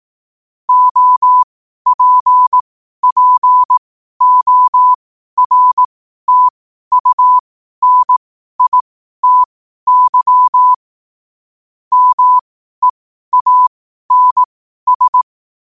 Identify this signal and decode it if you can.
Morse code, heard as a keyed tone, about 18 wpm, spaced as for 12 wpm, 1000 Hz. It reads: OPPORTUNITY MEANS
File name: qotdfiles / 12wpm